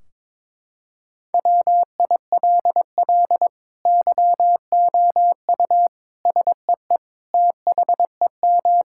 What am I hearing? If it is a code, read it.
Morse code, heard as a keyed tone, about 22 wpm, 700 Hz. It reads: WILL YOU SEE THEM
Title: Sentences from Top 1600 Words 22wpm